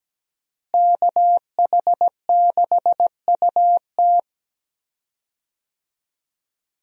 Morse code, heard as a keyed tone, about 17 wpm, 700 Hz. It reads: KH6UT
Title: USA Call Signs - 5 Characters - Repeated 2x - 17wpm